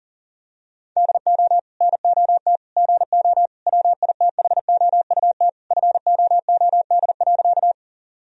Morse code, heard as a keyed tone, about 40 wpm, 700 Hz. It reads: DO NOT GO WITHOUT FOOD.
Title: Sentences from Top 1000 Words - Review Part 3 - 40wpm